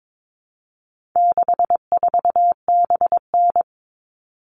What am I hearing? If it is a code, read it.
Morse code, heard as a keyed tone, about 22 wpm, 700 Hz. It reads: 64BN
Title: All Cut Numbers - 4-Digit 22wpm